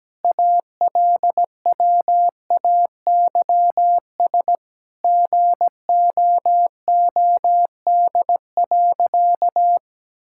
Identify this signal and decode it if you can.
Morse code, heard as a keyed tone, about 17 wpm, 700 Hz. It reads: ALWAYS GOOD.
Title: Sentences from Top 1000 Words - Review Part 5 - 17wpm